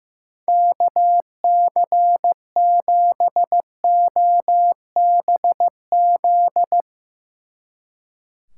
Morse code, heard as a keyed tone, about 15 wpm, 700 Hz. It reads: KC7OBZ
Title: Speed Racing - USA 2x3 Call Signs 15wpm